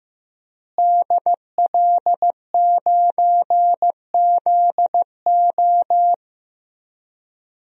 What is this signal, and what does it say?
Morse code, heard as a keyed tone, about 15 wpm, 700 Hz. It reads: DL9ZO
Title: DXCC-100 Call Signs - 5 Characters - Repeated 2x - 15wpm